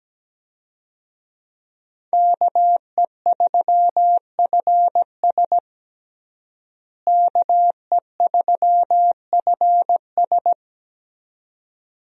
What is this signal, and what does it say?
Morse code, heard as a keyed tone, about 17 wpm, 700 Hz. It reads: KE3FS KE3FS
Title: USA Call Signs - 5 Characters - Repeated 2x - 17wpm